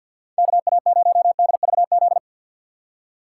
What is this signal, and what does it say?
Morse code, heard as a keyed tone, about 53 wpm, 700 Hz. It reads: KA0DVZ
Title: USA Call Signs - 6 Characters - 4x Character Spacing - 50wpm